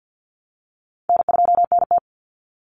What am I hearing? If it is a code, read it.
Morse code, heard as a keyed tone, about 49 wpm, 700 Hz. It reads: N2NT
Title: Call Signs - 50wpm